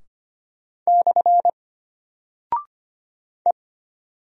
Morse code, heard as a keyed tone, about 25 wpm, 700 Hz. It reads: / E
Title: Single Character (Letter:Number:Punctuation:Prosign) Alt 25wpm